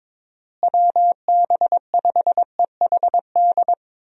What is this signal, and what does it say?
Morse code, heard as a keyed tone, about 22 wpm, 700 Hz. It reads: WB5EHD
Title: USA Call Signs - 6 Characters - Repeated 2x - 22wpm